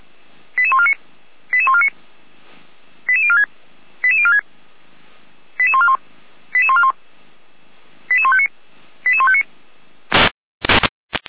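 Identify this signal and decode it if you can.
Five-tone selective call (ZVEI), heard as quick runs of five tones, tones of about 70 ms. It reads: 88149 88149 88046 88046 88141 88141 88159 88159
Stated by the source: ZVEI five-tone call 88194 received by SDR, logged by the Orlamünde fire brigade (feuerwehr-orlamuende).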